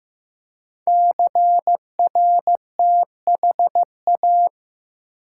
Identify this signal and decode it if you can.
Morse code, heard as a keyed tone, about 15 wpm, 700 Hz. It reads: CRTHA
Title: Code Groups - 5 Letters - 15wpm